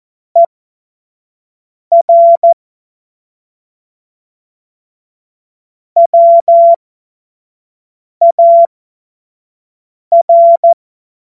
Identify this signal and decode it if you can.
Morse code, heard as a keyed tone, about 14 wpm, spaced as for 5 wpm, 680 Hz. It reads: ER WAR
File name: Morsekurs / Witze / Witz 15. November 2022 25 F70